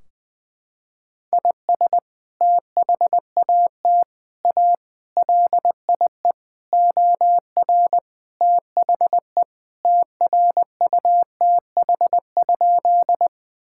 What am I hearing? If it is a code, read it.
Morse code, heard as a keyed tone, about 20 wpm, 700 Hz. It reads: IS THAT A LIE OR THE TRUTH?
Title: Sentences from Top 1600 Words 20wpm